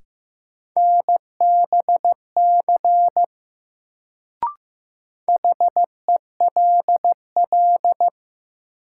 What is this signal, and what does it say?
Morse code, heard as a keyed tone, about 15 wpm, 700 Hz. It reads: NBC HELL
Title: Sets of 5 Words Encore - New Word Drill - 15wpm